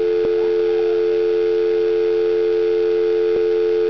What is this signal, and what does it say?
Dial tone (North America), 350 + 440 Hz.
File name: DIALTONE